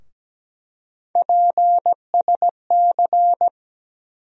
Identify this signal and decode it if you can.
Morse code, heard as a keyed tone, about 17 wpm, 700 Hz. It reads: PSC
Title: USA Call Sign Postfixes - 3 Letters 17wpm